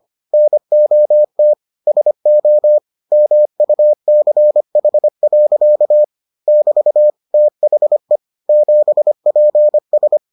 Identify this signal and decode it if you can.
Morse code, heard as a keyed tone, about 25 wpm, 600 Hz. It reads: NOT SO MUCH. <BT> THE 7PS